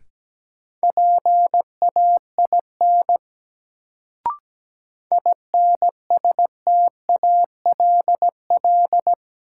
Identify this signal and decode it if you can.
Morse code, heard as a keyed tone, about 17 wpm, 700 Hz. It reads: PAIN INSTALL
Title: Sentences from Top 1600 Words - New Word Drill - 17wpm